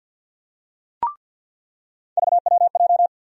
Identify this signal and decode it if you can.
Morse code, heard as a keyed tone, about 48 wpm, 700 Hz. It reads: VWJ
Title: USA Call Sign Postfixes - 3 Letters 50wpm